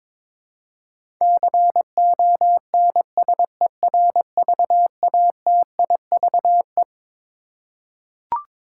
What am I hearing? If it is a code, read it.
Morse code, heard as a keyed tone, about 22 wpm, 700 Hz. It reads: CONSERVATIVE
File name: Twelve Letter Words - 22wpm